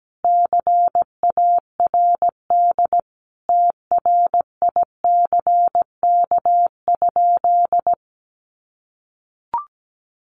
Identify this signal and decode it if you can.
Morse code, heard as a keyed tone, about 17 wpm, 700 Hz. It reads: CARD TRICK?
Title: Sentences from Top 1500 Words 17wpm